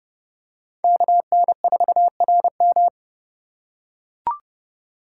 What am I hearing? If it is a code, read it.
Morse code, heard as a keyed tone, about 30 wpm, 700 Hz. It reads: KN4RM
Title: USA Call Signs - 5 Characters - 2x Character Spacing - 30wpm